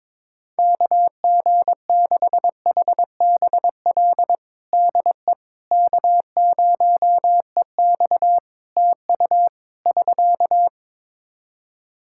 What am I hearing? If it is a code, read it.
Morse code, heard as a keyed tone, about 22 wpm, 700 Hz. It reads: KG6HBL DE K0EX TU <SK>
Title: K1USN SST QSO Practice 22wpm